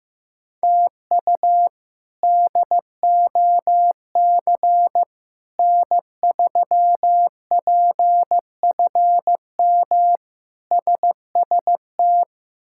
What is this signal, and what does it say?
Morse code, heard as a keyed tone, about 15 wpm, 700 Hz. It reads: TU DOC N3PFM SST
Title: K1USN SST QSO Practice - 15wpm